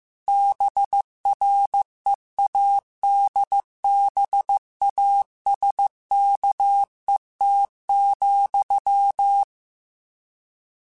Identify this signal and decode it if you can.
Morse code, heard as a keyed tone, about 15 wpm, 800 Hz. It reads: BREADBASKET,